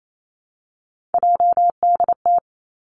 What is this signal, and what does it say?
Morse code, heard as a keyed tone, about 28 wpm, 700 Hz. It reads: JDT
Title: Three Challenging Letter-Number Combinations 28wpm